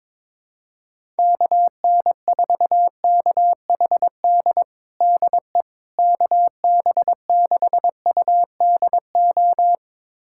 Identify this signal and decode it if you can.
Morse code, heard as a keyed tone, about 22 wpm, 700 Hz. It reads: KN4KHD DE KB6UDO
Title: USA Callsign DE Callsign 22wpm